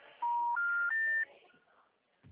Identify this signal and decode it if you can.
Special information tone.